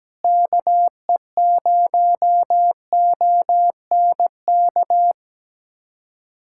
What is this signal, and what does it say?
Morse code, heard as a keyed tone, about 17 wpm, 700 Hz. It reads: KE0ONK
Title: USA Callsign DE Callsign - 6 Characters - 17wpm